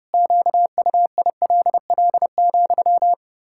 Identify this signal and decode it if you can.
Morse code, heard as a keyed tone, about 30 wpm, 700 Hz. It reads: QUILL,
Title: Pangram Challenge v2 - 30wpm